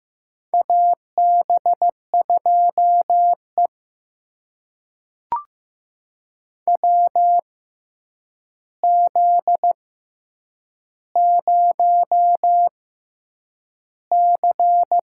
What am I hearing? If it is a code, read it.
Morse code, heard as a keyed tone, about 15 wpm, spaced as for 8 wpm, 700 Hz. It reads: AB2E WZ0C
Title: USA Call Signs - 4 Characters - 6x Character Spacing - 15wpm